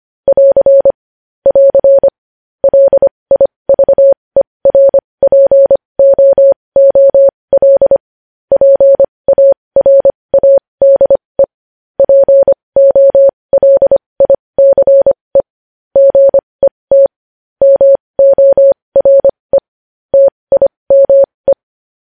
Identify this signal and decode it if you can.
Morse code, heard as a keyed tone, about 25 wpm, spaced as for 20 wpm, 550 Hz. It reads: <AR> <AR> LIVERPOOL PARADE POLICE GET MORE TIME
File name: podcast / News headlines-1x-25x20